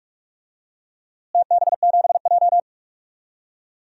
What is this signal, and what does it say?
Morse code, heard as a keyed tone, about 44 wpm, 700 Hz. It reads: TB7J
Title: DXCC-100 Call Signs - 4 Characters - Repeated 2x - 45wpm